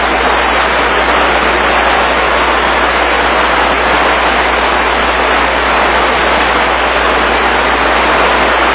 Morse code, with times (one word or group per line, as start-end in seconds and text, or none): none